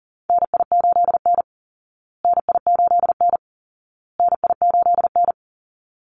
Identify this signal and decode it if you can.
Morse code, heard as a keyed tone, about 40 wpm, 700 Hz. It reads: NI8N NI8N NI8N
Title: Call Signs Repeated 2x - 40wpm